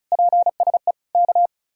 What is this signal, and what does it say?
Morse code, heard as a keyed tone, about 44 wpm, 700 Hz. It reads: PSE K
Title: Speed-Racing Common QSO Phrases - 35wpm